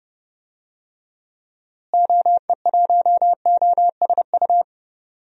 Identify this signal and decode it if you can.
Morse code, heard as a keyed tone, about 30 wpm, 700 Hz. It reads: OE1OSU
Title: DXCC-100 Call Signs - 6 Characters - Repeated 2x - 30wpm